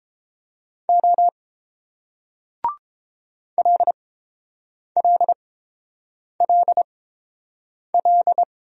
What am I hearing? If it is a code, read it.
Morse code, heard as a keyed tone, about 30 wpm, spaced as for 7 wpm, 700 Hz. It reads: O LLLL